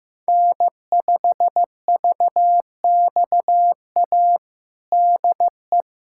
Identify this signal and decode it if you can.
Morse code, heard as a keyed tone, about 15 wpm, 700 Hz. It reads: N5VXA DE
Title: USA Callsign DE Callsign - 5 Characters - 15wpm